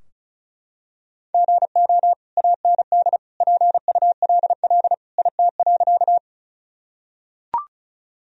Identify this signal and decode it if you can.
Morse code, heard as a keyed tone, about 34 wpm, 700 Hz. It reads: GO AND PULL IT.
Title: Sentences from Top 1000 Words - Review Part 2 - 35wpm